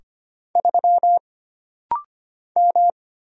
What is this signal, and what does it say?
Morse code, heard as a keyed tone, about 25 wpm, 700 Hz. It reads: 3 M
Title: Single Letter-Number Alt - Rapid-Fire 25wpm